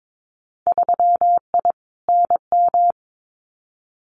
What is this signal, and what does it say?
Morse code, heard as a keyed tone, about 22 wpm, 700 Hz. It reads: 3I NM
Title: Winter Field Day Exchanges Encore 22wpm